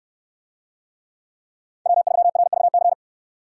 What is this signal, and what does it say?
Morse code, heard as a keyed tone, about 67 wpm, 700 Hz. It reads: W3RVC